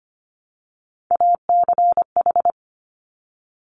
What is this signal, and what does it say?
Morse code, heard as a keyed tone, about 25 wpm, 700 Hz. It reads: ACH
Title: Top Three Letter Combinations in Words 25wpm